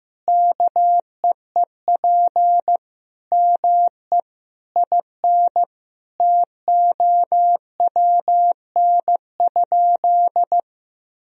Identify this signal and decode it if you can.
Morse code, heard as a keyed tone, about 16 wpm, 700 Hz. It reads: KEEP ME IN TOWN?
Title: Sentences from Top 500 Words and Abbreviations 15wpm